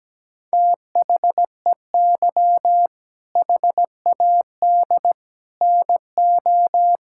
Morse code, heard as a keyed tone, about 17 wpm, 700 Hz. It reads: THEY HAD NO
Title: Sentences from Top 1000 Words - Review Part 6 - 17wpm